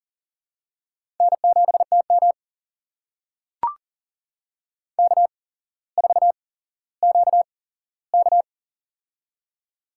Morse code, heard as a keyed tone, about 37 wpm, spaced as for 17 wpm, 700 Hz. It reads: N7TM K4QK
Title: USA Call Signs - 4 Characters - 8x Character Spacing - 40wpm